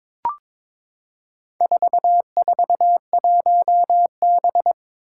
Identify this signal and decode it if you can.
Morse code, heard as a keyed tone, about 22 wpm, 700 Hz. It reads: 441B
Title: Four Challenging Letter-Number Combinations 22wpm